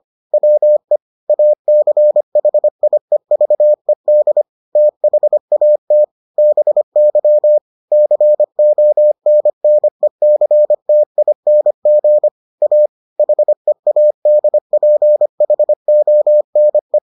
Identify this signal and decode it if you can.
Morse code, heard as a keyed tone, about 25 wpm, 600 Hz. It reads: WE ACHIEVED THAT BY CONNECTING A HEADPHONE